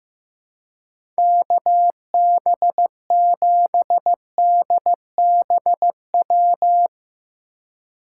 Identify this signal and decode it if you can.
Morse code, heard as a keyed tone, about 15 wpm, 700 Hz. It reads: KB7DBW
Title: USA Call Signs - 6 Characters - 2x Character Spacing - 15wpm